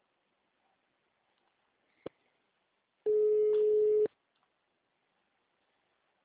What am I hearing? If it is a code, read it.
Ringback tone (Europe, 425 Hz), with speech.